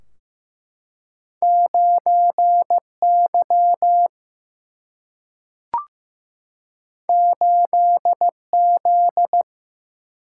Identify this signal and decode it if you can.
Morse code, heard as a keyed tone, about 15 wpm, 700 Hz. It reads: 9 Y 8 Z
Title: Single-Letter Single-Number 15wpm